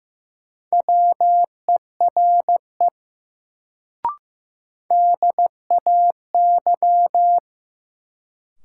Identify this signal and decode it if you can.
Morse code, heard as a keyed tone, about 16 wpm, 700 Hz. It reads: WERE DAY
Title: Sets of 2 Words - New Word Drill - 15wpm